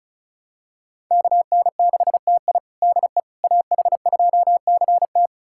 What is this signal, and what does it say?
Morse code, heard as a keyed tone, about 36 wpm, 690 Hz. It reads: KN6TI DE AH2CT
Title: USA Callsign DE Callsign - 5 Characters - 35wpm